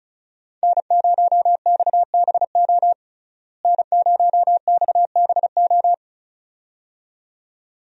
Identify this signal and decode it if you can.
Morse code, heard as a keyed tone, about 36 wpm, 700 Hz. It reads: N0XBO N0XBO
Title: USA Call Signs - 5 Characters - Repeated 2x - 35wpm